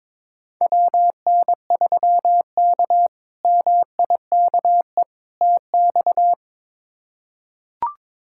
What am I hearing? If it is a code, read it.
Morse code, heard as a keyed tone, about 22 wpm, 700 Hz. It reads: WN3K MIKE TX
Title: K1USN SST QSO Practice - 22wpm